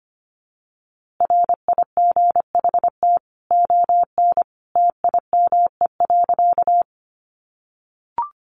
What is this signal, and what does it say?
Morse code, heard as a keyed tone, about 25 wpm, 700 Hz. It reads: RIGHT ON TIME.
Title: A Sound Of Thunder 25wpm